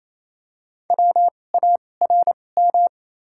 Morse code, heard as a keyed tone, about 28 wpm, spaced as for 20 wpm, 690 Hz. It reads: WARM
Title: 2x Character Spacing - Top 500 Words - 28wpm